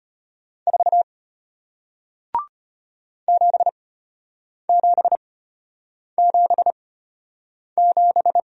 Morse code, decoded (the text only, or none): 4 7777